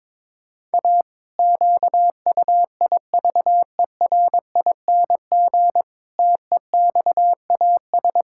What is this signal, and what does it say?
Morse code, heard as a keyed tone, about 22 wpm, 700 Hz. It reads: A QUIVERING TEXAS
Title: Pangram Challenge v2 - 22wpm